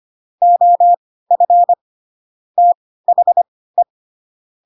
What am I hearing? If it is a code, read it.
Morse code, heard as a keyed tone, about 25 wpm, spaced as for 16 wpm, 700 Hz. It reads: OF THE
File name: Pinocchio / Pinocchio ch5 10 25 ik1hge